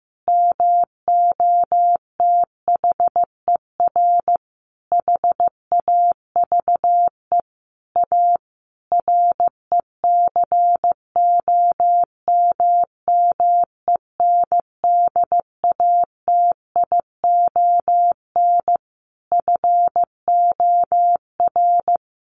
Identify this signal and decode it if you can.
Morse code, heard as a keyed tone, about 15 wpm, 700 Hz. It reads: MOTHER HAVE A RECOMMENDATION FOR
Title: Sentences from Top 1400 Words 15wpm